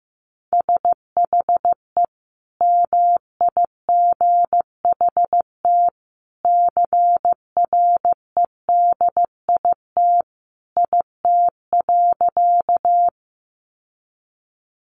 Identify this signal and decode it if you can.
Morse code, heard as a keyed tone, about 15 wpm, 700 Hz. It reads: SHE MIGHT CREDIT IT.
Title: Sentences from Top 1000 Words 15wpm